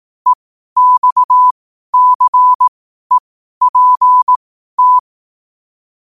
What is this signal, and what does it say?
Morse code, heard as a keyed tone, about 18 wpm, spaced as for 12 wpm, 1000 Hz. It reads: EXCEPT
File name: qotdfiles / 12wpm